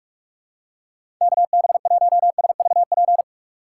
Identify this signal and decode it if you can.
Morse code, heard as a keyed tone, about 44 wpm, 700 Hz. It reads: KB1SVP
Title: USA Call Signs - 6x Character Spacing - 45wpm